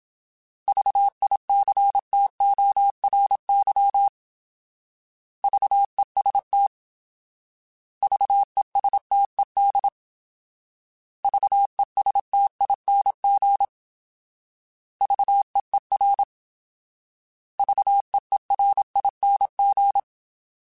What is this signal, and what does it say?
Morse code, heard as a keyed tone, about 27 wpm, 800 Hz. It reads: VICTORY VEST VESTED VESTING VEER VEERING